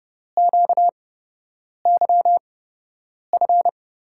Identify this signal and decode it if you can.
Morse code, heard as a keyed tone, about 30 wpm, spaced as for 8 wpm, 700 Hz. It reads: QYF
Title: USA Call Signs - 6 Characters - 8x Character Spacing - 30wpm